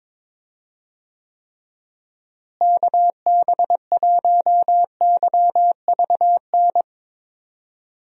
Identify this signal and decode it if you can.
Morse code, heard as a keyed tone, about 22 wpm, 700 Hz. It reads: KB1YVN
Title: USA Call Signs - 6 Characters - Repeated 2x - 22wpm